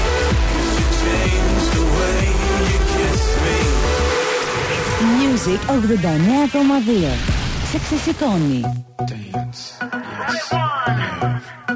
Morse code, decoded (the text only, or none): none